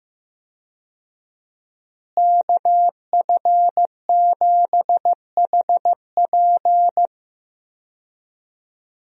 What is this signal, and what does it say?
Morse code, heard as a keyed tone, about 15 wpm, 700 Hz. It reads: KF7HP
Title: USA Call Signs - 5 Characters - Repeated 3x - 15wpm